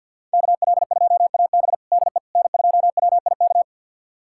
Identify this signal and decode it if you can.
Morse code, heard as a keyed tone, about 46 wpm, 700 Hz. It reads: KL2AB DE N2WIK
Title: USA Callsign DE Callsign - 5 Characters - 50wpm